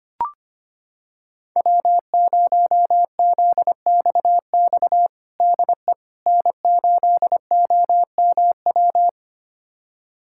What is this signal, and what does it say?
Morse code, heard as a keyed tone, about 25 wpm, 700 Hz. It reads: W0ZXX DE N8OMW
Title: USA Callsign DE Callsign - 5 Characters - 25wpm